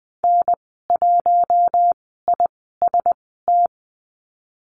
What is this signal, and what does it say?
Morse code, heard as a keyed tone, about 20 wpm, spaced as for 14 wpm, 700 Hz. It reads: N1IST
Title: 042 Lesson 8g - Number 1 - Call Signs 20wpm